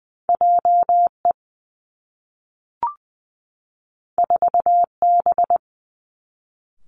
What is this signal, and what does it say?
Morse code, heard as a keyed tone, about 20 wpm, 700 Hz. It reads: J E 4 B
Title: Two Challenging Letter-Number Combinations 20wpm